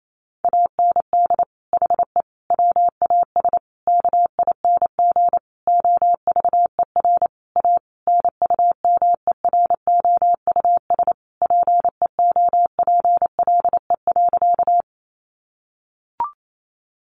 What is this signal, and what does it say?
Morse code, heard as a keyed tone, about 28 wpm, 700 Hz. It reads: AND HE WAS KING OVER A NUMEROUS PEOPLE.
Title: Sentences from Top 1400 Words 28wpm